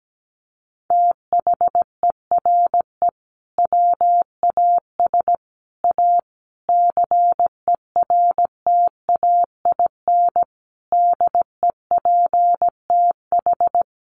Morse code, heard as a keyed tone, about 17 wpm, 700 Hz. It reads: THERE WAS A CERTAIN DEPTH
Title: Sentences from Top 1200 Words 17wpm